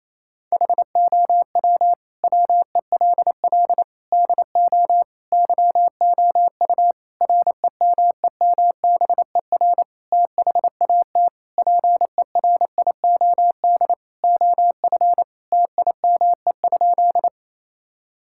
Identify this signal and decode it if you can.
Morse code, heard as a keyed tone, about 28 wpm, 700 Hz. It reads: HOW WELL DO YOU REMEMBER THAT PERIOD OF TIME?
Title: Sentences from Top 1000 Words - Review Part 5 - 28wpm